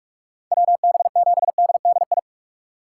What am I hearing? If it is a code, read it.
Morse code, heard as a keyed tone, about 46 wpm, 700 Hz. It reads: WB7DDI
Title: USA Call Signs - 6 Characters - 2x Character Spacing - 45wpm